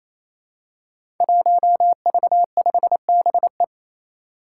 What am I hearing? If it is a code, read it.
Morse code, heard as a keyed tone, about 28 wpm, 700 Hz. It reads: 1V5BE
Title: Five Challenging Letter-Number Combinations 28wpm